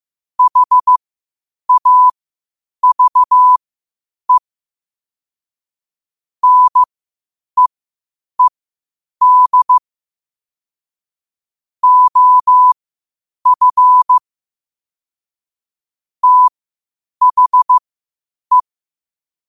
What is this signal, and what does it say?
Morse code, heard as a keyed tone, about 15 wpm, spaced as for 8 wpm, 1000 Hz. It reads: HAVE NEED OF THE
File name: qotdfiles / 8wpm